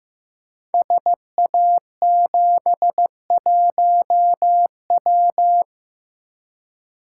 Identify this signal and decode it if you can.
Morse code, heard as a keyed tone, about 15 wpm, 700 Hz. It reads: SA71W
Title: Code Groups - 5 Letter:Numbers - 15wpm